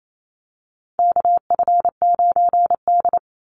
Morse code, heard as a keyed tone, about 28 wpm, 700 Hz. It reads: KF9D
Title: Call Signs Repeated 2x - 28wpm